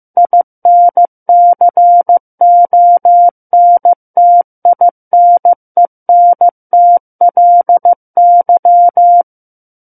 Morse code, heard as a keyed tone, about 15 wpm, 700 Hz. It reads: INCONTINENTLY
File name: Book One / 06-Chapter 6 - 15 WPM